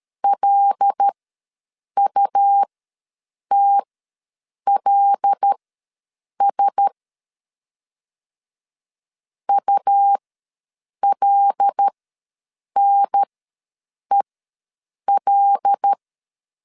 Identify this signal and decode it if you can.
Morse code, heard as a keyed tone, about 13 wpm, spaced as for 7 wpm, 800 Hz. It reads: LUTLS ULNEL
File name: LEZIONI / 5 LBVC / 5 L ws 12wpm 2958